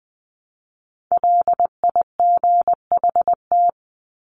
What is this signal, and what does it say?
Morse code, heard as a keyed tone, about 20 wpm, 700 Hz. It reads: LIGHT
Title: Top 300 Words 20wpm